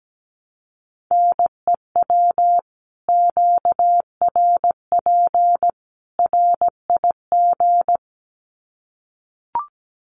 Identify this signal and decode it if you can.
Morse code, heard as a keyed tone, about 17 wpm, 700 Hz. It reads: NEW QRP RIG
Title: Common QSO Phrases 17wpm